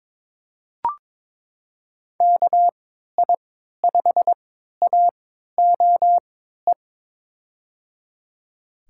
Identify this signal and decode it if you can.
Morse code, heard as a keyed tone, about 22 wpm, spaced as for 13 wpm, 700 Hz. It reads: KI5AOE
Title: USA Call Signs - 6 Characters - 3x Character Spacing - 22wpm